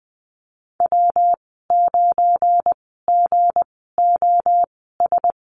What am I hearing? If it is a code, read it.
Morse code, heard as a keyed tone, about 20 wpm, spaced as for 15 wpm, 700 Hz. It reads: W9GOS
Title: 179 Lesson 29g - Slash - Call Signs 20wpm